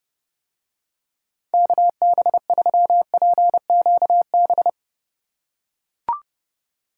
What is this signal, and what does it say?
Morse code, heard as a keyed tone, about 31 wpm, 700 Hz. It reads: KB3PQB